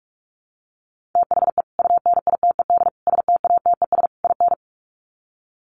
Morse code, heard as a keyed tone, about 44 wpm, 700 Hz. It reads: THE UNITED STATES IN
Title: Sets of 4 Words 45wpm